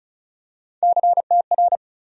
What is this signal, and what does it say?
Morse code, heard as a keyed tone, about 36 wpm, 700 Hz. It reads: CTR
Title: USA Call Sign Postfixes - 3 Letters 35wpm